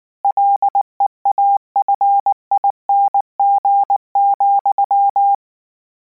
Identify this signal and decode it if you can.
Morse code, heard as a keyed tone, about 19 wpm, 800 Hz. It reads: LEAFING,